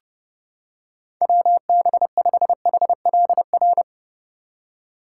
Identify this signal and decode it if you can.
Morse code, heard as a keyed tone, about 29 wpm, 700 Hz. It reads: WB5HLR